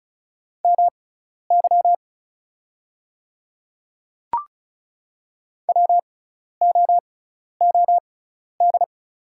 Morse code, heard as a keyed tone, about 36 wpm, spaced as for 12 wpm, 700 Hz. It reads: MY WOOD
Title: 6x Character Spacing - Top 300 Words - 35wpm